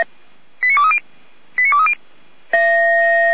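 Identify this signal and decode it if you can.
Five-tone selective call (ZVEI), heard as quick runs of five tones, tones of about 70 ms. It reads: 89230 89230
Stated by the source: ZVEI five-tone call received by SDR, from audio.f85360.feuerwehr-orlamuende.org (fire brigade Orlamünde).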